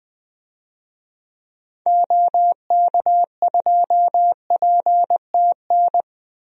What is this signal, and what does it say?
Morse code, heard as a keyed tone, about 20 wpm, 700 Hz. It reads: OK2PTN
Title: DXCC-100 Call Signs - 6 Characters - 20wpm